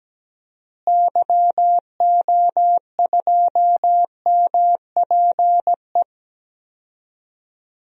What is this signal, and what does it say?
Morse code, heard as a keyed tone, about 17 wpm, 700 Hz. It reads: YO2MPE
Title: DXCC-100 Call Signs - 6 Characters - Repeated 2x - 17wpm